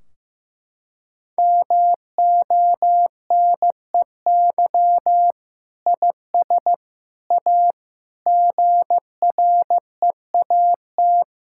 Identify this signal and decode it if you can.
Morse code, heard as a keyed tone, about 15 wpm, 700 Hz. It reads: MONEY IS A GREAT
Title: Sentences from Top 1600 Words 15wpm